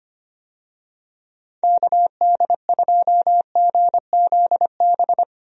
Morse code, heard as a keyed tone, about 25 wpm, 700 Hz. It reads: KD2GZB
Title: USA Call Signs - 6 Characters - 25wpm